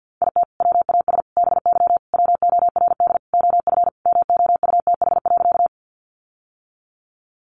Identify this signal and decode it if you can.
Morse code, heard as a keyed tone, about 50 wpm, 700 Hz. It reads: IT WAS BY WORD OF MOUTH.
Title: Sentences from Top 1500 Words 50wpm